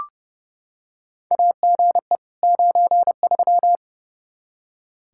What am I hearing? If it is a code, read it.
Morse code, heard as a keyed tone, about 30 wpm, 700 Hz. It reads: AGE 93
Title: K1USN SST QSO Practice 30wpm